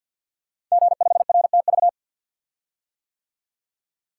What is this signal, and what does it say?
Morse code, heard as a keyed tone, about 50 wpm, 700 Hz. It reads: G5RTV
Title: Code Groups - 5 Letter:Numbers - 50wpm